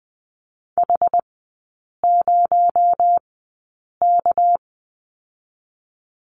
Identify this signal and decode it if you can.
Morse code, heard as a keyed tone, about 20 wpm, 700 Hz. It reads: H 0 K
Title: 218 Lesson 35e - Number 0 - Three Characters 20wpm